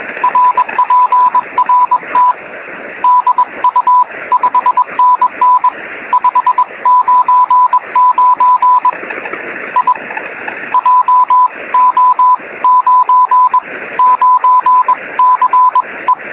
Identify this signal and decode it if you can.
Morse code, heard as a keyed tone, about 21 wpm, 1000 Hz. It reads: RPRT DU 5NN 599 I JO99CE
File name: sm7lcb / 09-sm0nue 23cm